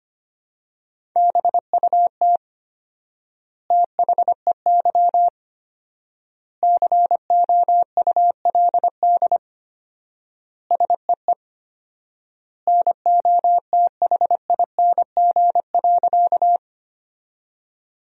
Morse code, heard as a keyed tone, about 25 wpm, 700 Hz. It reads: BUT THEY COULD SEE NOTHING.
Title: 4x Word-Spacing - Sentences from Top 500 Words - 25wpm